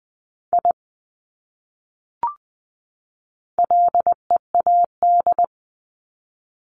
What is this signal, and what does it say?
Morse code, heard as a keyed tone, about 19 wpm, 700 Hz. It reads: I LEAD
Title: Top 500 Words 20wpm